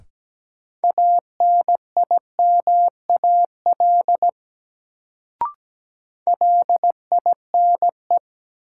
Morse code, heard as a keyed tone, about 17 wpm, 700 Hz. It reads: ANIMAL LINE
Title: Sentences from Top 1000 Words - Review Part 1 - New Word Drill - 17wpm